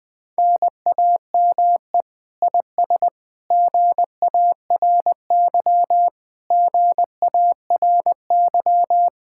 Morse code, heard as a keyed tone, about 20 wpm, 700 Hz. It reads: NAME IS GARY GARY